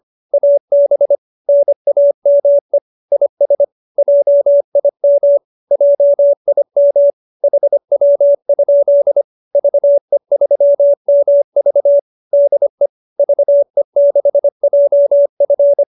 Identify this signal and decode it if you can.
Morse code, heard as a keyed tone, about 25 wpm, 570 Hz. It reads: AB NAME IS JIM JIM HW? VE3MV DE VE6JF
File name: QSO303 25